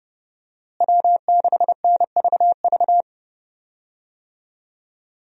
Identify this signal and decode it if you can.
Morse code, heard as a keyed tone, about 30 wpm, 700 Hz. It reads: W6NVV
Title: USA Call Signs - 5 Characters - Repeated 3x - 30wpm